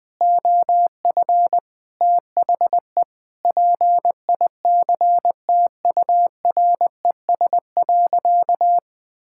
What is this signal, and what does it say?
Morse code, heard as a keyed tone, about 20 wpm, 700 Hz. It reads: OF THE PICTURES.
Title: Sentences with 7-Word Phrases - Spoken Beginning - 20wpm